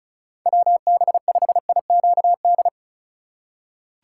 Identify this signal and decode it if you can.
Morse code, heard as a keyed tone, about 36 wpm, 700 Hz. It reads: WB5IQD
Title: USA Call Signs - 6 Characters - 3x Character Spacing - 35wpm